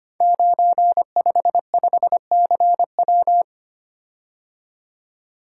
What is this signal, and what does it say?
Morse code, heard as a keyed tone, about 25 wpm, 700 Hz. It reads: 955CW
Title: Code Groups - 5 Letter:Numbers - 25wpm